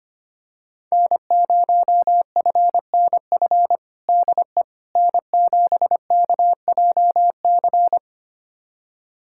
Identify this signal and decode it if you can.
Morse code, heard as a keyed tone, about 25 wpm, 700 Hz. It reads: N0FNF DE N7KJC
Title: USA Callsign DE Callsign - 5 Characters - 25wpm